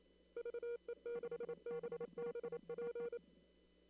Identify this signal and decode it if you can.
Morse code, heard as a keyed tone, about 25 wpm, 470 Hz. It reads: VE6BBP
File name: CW / kt5j wpx cw-op-k5pi